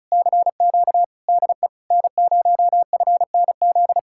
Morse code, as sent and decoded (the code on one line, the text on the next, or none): -.-. --.- / -.. / -. ----- ..-. -. --..
CQ DE N0FNZ